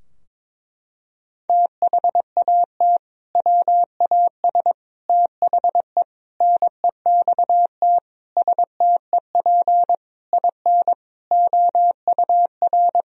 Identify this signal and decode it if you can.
Morse code, heard as a keyed tone, about 22 wpm, 700 Hz. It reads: THAT WAS THE NEXT STEP IN OUR
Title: Sentences from Top 1000 Words - Review Part 4 - 22wpm